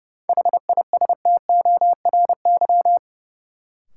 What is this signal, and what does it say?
Morse code, heard as a keyed tone, about 30 wpm, 700 Hz. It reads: HISTORY
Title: Sentences from Top 600 Words - New Word Drill - 30wpm